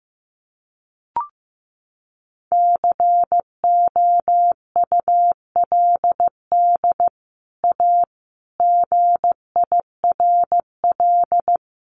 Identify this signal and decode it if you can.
Morse code, heard as a keyed tone, about 15 wpm, 700 Hz. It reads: COULD A GIRL
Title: Sentences from Top 1200 Words 15wpm